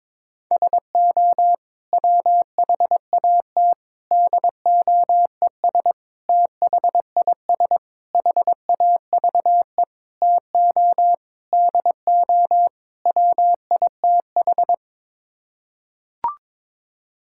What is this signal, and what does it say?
Morse code, heard as a keyed tone, about 22 wpm, 700 Hz. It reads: SO WHAT DOES THIS HAVE TO DO WITH
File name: practice / 22wpm / Sets of 8 Words - 22wpm